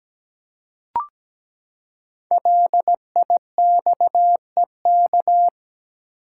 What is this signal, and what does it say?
Morse code, heard as a keyed tone, about 17 wpm, 700 Hz. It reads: LIXEK
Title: Code Groups - 5 Letter:Numbers - 17wpm